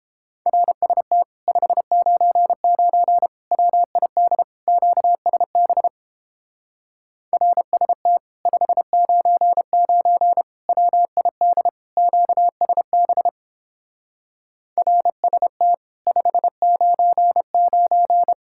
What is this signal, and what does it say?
Morse code, heard as a keyed tone, about 29 wpm, 700 Hz. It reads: RST 599 WID QSB RST 599 WID QSB RST 599